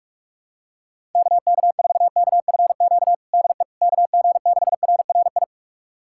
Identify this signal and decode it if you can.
Morse code, heard as a keyed tone, about 46 wpm, 700 Hz. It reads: KK4KFQ DE KG6RRI
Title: USA Callsign DE Callsign - 6 Characters - 45wpm